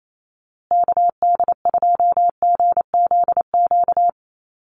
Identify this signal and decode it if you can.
Morse code, heard as a keyed tone, about 28 wpm, 700 Hz. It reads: KD2GZQ
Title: USA Call Signs Repeated 3x 28wpm